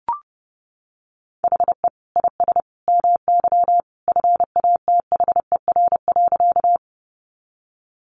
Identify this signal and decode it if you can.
Morse code, heard as a keyed tone, about 30 wpm, 700 Hz. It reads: HE IS MY FATHER.
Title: Sentences from Top 300 Words 30wpm